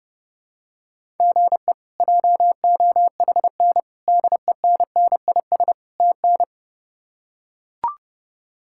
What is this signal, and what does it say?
Morse code, heard as a keyed tone, about 30 wpm, 700 Hz. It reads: GE JOHN DENNIS TN
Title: K1USN SST QSO Practice - 30wpm